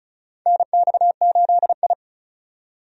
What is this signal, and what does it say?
Morse code, heard as a keyed tone, about 34 wpm, 700 Hz. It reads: NX8I
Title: USA Call Signs - 4 Characters - 35wpm